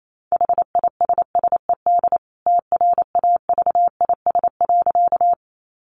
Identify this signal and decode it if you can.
Morse code, heard as a keyed tone, about 28 wpm, 700 Hz. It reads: HISSED TRAVIS.
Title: A Sound Of Thunder 28wpm